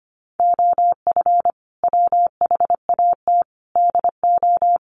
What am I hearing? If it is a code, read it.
Morse code, heard as a keyed tone, about 25 wpm, 700 Hz. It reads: OF WHAT DO
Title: Sentences from Top 1300 Words 25wpm